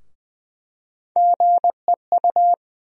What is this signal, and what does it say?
Morse code, heard as a keyed tone, about 20 wpm, 700 Hz. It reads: GEU
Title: USA Call Sign Postfixes - 3 Letters 20wpm